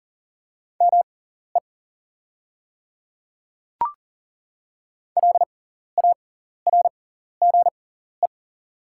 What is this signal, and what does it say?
Morse code, heard as a keyed tone, about 40 wpm, spaced as for 14 wpm, 700 Hz. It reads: ME LARGE
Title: 6x Character Spacing - Top 500 Words - 40wpm